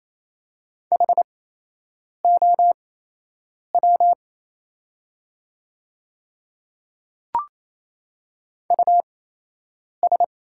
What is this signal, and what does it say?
Morse code, heard as a keyed tone, about 27 wpm, spaced as for 8 wpm, 700 Hz. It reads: HOW US
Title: 8x Character Spacing - Top 200 Words - 28wpm